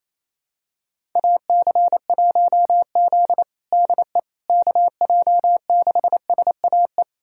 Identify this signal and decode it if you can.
Morse code, heard as a keyed tone, about 28 wpm, 700 Hz. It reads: AC1Z DE KJ6SAE
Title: USA Callsign DE Callsign 28wpm